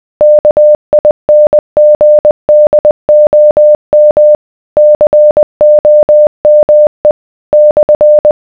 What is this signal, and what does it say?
Morse code, heard as a keyed tone, about 20 wpm, 600 Hz. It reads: KINGDOM COME /